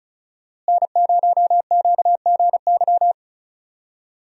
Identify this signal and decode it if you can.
Morse code, heard as a keyed tone, about 34 wpm, 700 Hz. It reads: N0QGY
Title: USA Call Signs - 5 Characters - Repeated 2x - 35wpm